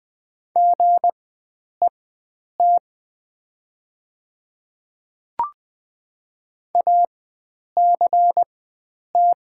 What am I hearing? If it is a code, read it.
Morse code, heard as a keyed tone, about 20 wpm, spaced as for 9 wpm, 700 Hz. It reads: GET ACT
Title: 4x Character Spacing - Top 500 Words - 20wpm